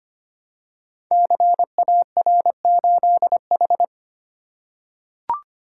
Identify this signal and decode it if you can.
Morse code, heard as a keyed tone, about 25 wpm, 700 Hz. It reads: CAR8H
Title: Code Groups - 5 Letter:Numbers - 25wpm